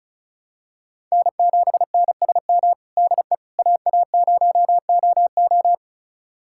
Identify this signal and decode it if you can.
Morse code, heard as a keyed tone, about 36 wpm, 700 Hz. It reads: N7NSM DE AA0OO
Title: USA Callsign DE Callsign - 5 Characters - 35wpm